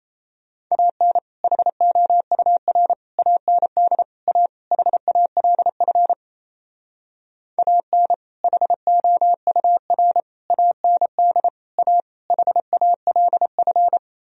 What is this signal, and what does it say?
Morse code, heard as a keyed tone, about 35 wpm, 700 Hz. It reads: AN HOUR AND A HALF AN HOUR AND A HALF
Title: Speed-Racing - Sets of 5 Words 28wpm